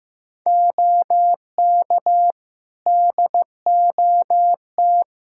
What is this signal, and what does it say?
Morse code, heard as a keyed tone, about 15 wpm, 700 Hz. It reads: OK DOT
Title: Practice QSO - Exchange 3 - More Info - 15wpm